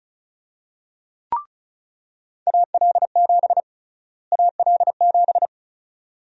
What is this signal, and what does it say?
Morse code, heard as a keyed tone, about 34 wpm, 700 Hz. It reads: AL7 AL7
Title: USA Call Sign Prefixes - Repeated 2x - 35wpm